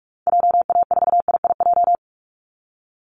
Morse code, heard as a keyed tone, about 44 wpm, 700 Hz. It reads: JA4IIJ
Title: Call Signs Repeated 3x - 45wpm